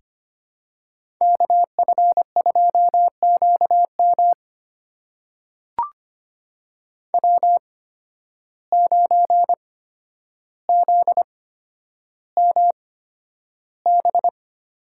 Morse code, heard as a keyed tone, about 25 wpm, spaced as for 11 wpm, 700 Hz. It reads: KF2QM W9ZMB
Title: USA Call Signs - 5 Characters - 8x Character Spacing - 25wpm